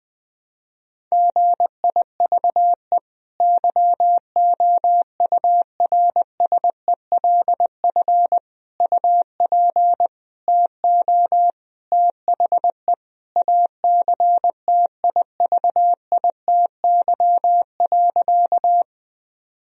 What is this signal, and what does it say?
Morse code, heard as a keyed tone, about 20 wpm, 700 Hz. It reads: GIVE YOURSELF UP TO THE ACTIVITY.
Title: Sentences from Top 1000 Words - Review Part 4 - 20wpm